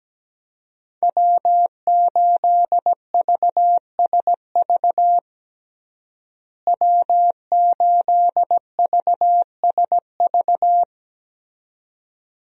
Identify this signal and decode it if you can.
Morse code, heard as a keyed tone, about 17 wpm, 700 Hz. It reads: W8VSV W8VSV
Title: USA Call Signs - 5 Characters - Repeated 2x - 17wpm